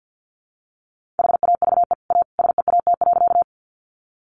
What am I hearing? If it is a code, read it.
Morse code, heard as a keyed tone, about 50 wpm, 700 Hz. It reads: HAVE A SEAT.
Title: Sentences from Top 1500 Words 50wpm